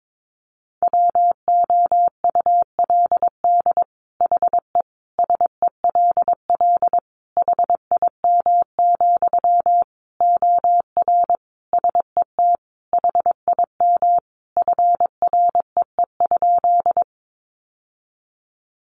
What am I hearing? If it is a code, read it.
Morse code, heard as a keyed tone, about 22 wpm, 700 Hz. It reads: WOULD HE SELL HIM, OR SET HIM FREE?
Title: Sentences from Top 1100 Words 22wpm